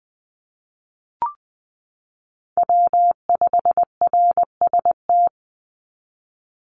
Morse code, heard as a keyed tone, about 20 wpm, 690 Hz. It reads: W5RST
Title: NA SOTA Chaser Callsigns 20wpm